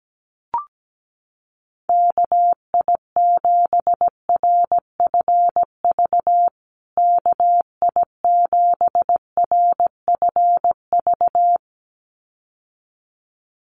Morse code, heard as a keyed tone, about 17 wpm, 700 Hz. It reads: KI7RFV KI7RFV
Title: USA Call Signs Repeated 2x 17wpm